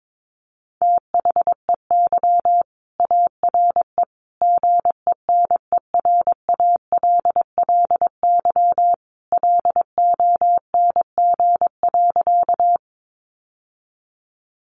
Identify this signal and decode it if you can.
Morse code, heard as a keyed tone, about 22 wpm, 700 Hz. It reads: THEY ARE GENERALLY LONG.
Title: Sentences from Top 600 Words 22wpm